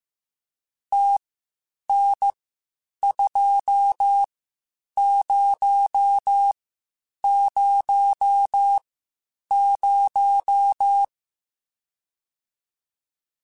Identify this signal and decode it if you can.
Morse code, heard as a keyed tone, about 15 wpm, spaced as for 8 wpm, 800 Hz. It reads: TN2000
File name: (spaced) / QSOs Spaced-mp3s / WD0FBT 1 @13 sp